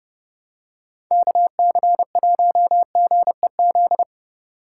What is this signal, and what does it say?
Morse code, heard as a keyed tone, about 30 wpm, 700 Hz. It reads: KC1GEZ
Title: USA Call Signs - 4x Character Spacing - 30wpm